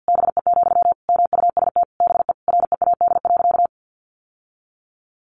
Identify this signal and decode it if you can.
Morse code, heard as a keyed tone, about 49 wpm, 700 Hz. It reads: BE, MUST BE READ.
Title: Sentences from Top 1100 Words 50wpm